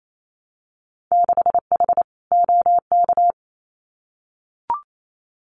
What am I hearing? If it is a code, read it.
Morse code, heard as a keyed tone, about 28 wpm, 700 Hz. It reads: 6H OK
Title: Winter Field Day Exchanges Encore 28wpm